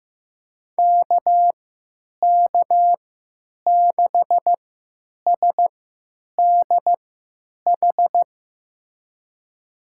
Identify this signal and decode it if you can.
Morse code, heard as a keyed tone, about 15 wpm, spaced as for 9 wpm, 700 Hz. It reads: KK6SDH
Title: USA Call Signs - 6 Characters - 3x Character Spacing - 15wpm